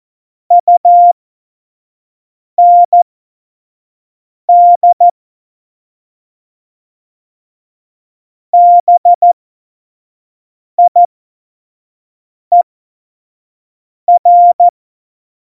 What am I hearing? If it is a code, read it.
Morse code, heard as a keyed tone, about 14 wpm, spaced as for 5 wpm, 700 Hz. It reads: UND BIER
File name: Morsekurs / Witze / Witz 01. Oktober 2019, 25 F70